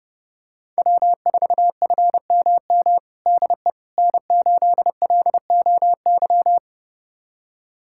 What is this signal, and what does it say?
Morse code, heard as a keyed tone, about 31 wpm, 700 Hz. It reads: W4FMM DE N8LOY
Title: USA Callsign DE Callsign - 5 Characters - 30wpm